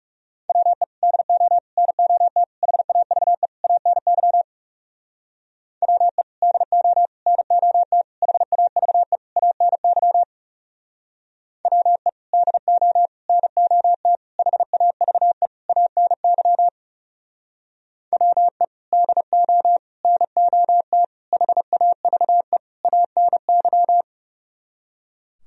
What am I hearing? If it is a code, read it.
Morse code, heard as a keyed tone, about 40 wpm, 700 Hz. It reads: WE DO NOT HAVE ANY WE DO NOT HAVE ANY WE DO NOT HAVE ANY WE DO NOT HAVE ANY